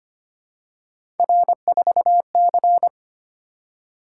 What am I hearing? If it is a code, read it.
Morse code, heard as a keyed tone, about 25 wpm, 700 Hz. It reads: R4C
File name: Three Letter-Number Combinations - 25wpm